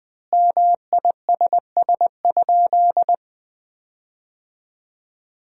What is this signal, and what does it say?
Morse code, heard as a keyed tone, about 20 wpm, 700 Hz. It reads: MISS?
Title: 4x Word-Spacing - Sentences from Top 500 Words - 20wpm